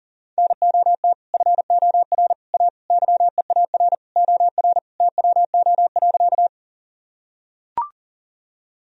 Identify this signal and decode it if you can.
Morse code, heard as a keyed tone, about 40 wpm, 700 Hz. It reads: NOT FOR A YEAR OR TWO.
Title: Sentences from Top 1000 Words - Review Part 1 - 40wpm